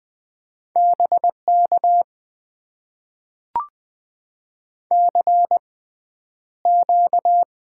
Morse code, heard as a keyed tone, about 20 wpm, spaced as for 10 wpm, 700 Hz. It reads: BK CQ
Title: Top 90 QSO Elements - 6x Character Spacing - 20wpm